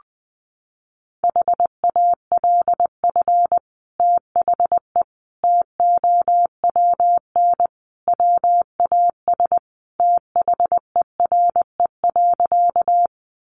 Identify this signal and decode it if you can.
Morse code, heard as a keyed tone, about 20 wpm, 700 Hz. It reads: HALF THE TOWN WAS THERE.
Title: Sentences from Top 500 Words 20wpm